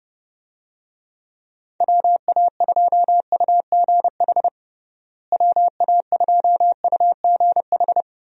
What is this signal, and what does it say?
Morse code, heard as a keyed tone, about 30 wpm, 700 Hz. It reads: WA2UGH WA2UGH
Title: USA Call Signs - 6 Characters - Repeated 2x - 30wpm